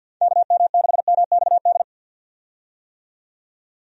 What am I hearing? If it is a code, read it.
Morse code, heard as a keyed tone, about 47 wpm, 700 Hz. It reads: KM6MXD
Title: USA Call Signs - 6 Characters - 50wpm